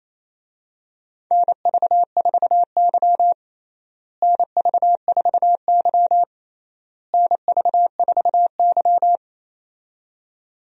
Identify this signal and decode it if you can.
Morse code, heard as a keyed tone, about 28 wpm, 700 Hz. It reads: NV4Y NV4Y NV4Y
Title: USA Call Signs - 4 Characters - Repeated 3x - 28wpm